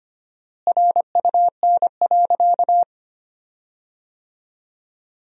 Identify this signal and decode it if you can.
Morse code, heard as a keyed tone, about 25 wpm, 700 Hz. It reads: RUN.
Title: 4x Word-Spacing - Sentences from Top 500 Words - 25wpm